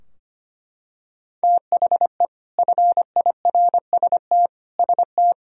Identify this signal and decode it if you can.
Morse code, heard as a keyed tone, about 25 wpm, 700 Hz. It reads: THE FIRST ST
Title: Sentences from Top 1000 Words - Review Part 3 - 25wpm